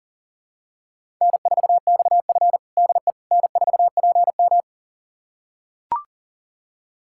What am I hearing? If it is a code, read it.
Morse code, heard as a keyed tone, about 40 wpm, 700 Hz. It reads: N4XF DE N4PM
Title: USA Callsign DE Callsign - 4 Characters - 40wpm